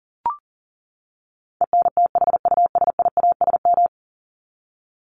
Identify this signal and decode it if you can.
Morse code, heard as a keyed tone, about 40 wpm, 700 Hz. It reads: ENTHUSIASM
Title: Top 1751-2000 Words 40wpm